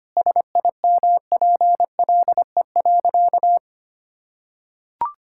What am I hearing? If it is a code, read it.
Morse code, heard as a keyed tone, about 25 wpm, 700 Hz. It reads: SIMPLE.
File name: practice / 25wpm / Sentences with 3-Word Phrases - 25wpm